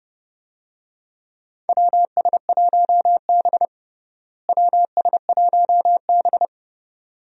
Morse code, heard as a keyed tone, about 30 wpm, 700 Hz. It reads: WS1B WS1B
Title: USA Call Signs - 4 Characters - Repeated 2x - 30wpm